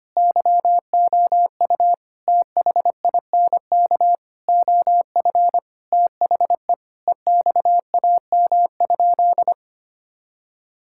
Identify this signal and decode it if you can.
Morse code, heard as a keyed tone, about 25 wpm, 700 Hz. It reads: YOU THINK OF THE EXAM?
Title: Sentences from Top 1000 Words - Review Part 5 - 25wpm